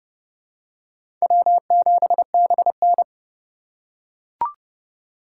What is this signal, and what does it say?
Morse code, heard as a keyed tone, about 31 wpm, 700 Hz. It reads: W7BN
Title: USA Call Signs - 4 Characters - 6x Character Spacing - 30wpm